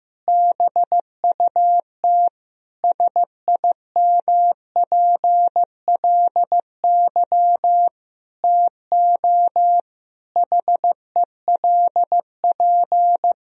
Morse code, heard as a keyed tone, about 15 wpm, 700 Hz. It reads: BUT SIMPLY TO HELP
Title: Sentences from Top 1000 Words - Review Part 4 - 15wpm